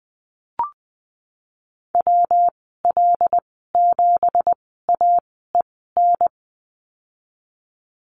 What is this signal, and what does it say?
Morse code, heard as a keyed tone, about 20 wpm, spaced as for 15 wpm, 700 Hz. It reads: WL7AEN
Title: 167 Lesson 27g - Number 7 - Call Signs 20wpm